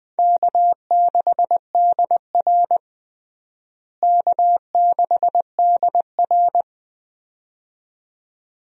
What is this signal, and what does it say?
Morse code, heard as a keyed tone, about 20 wpm, 700 Hz. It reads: K6DR K6DR
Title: USA Call Signs - 4 Characters - Repeated 2x - 20wpm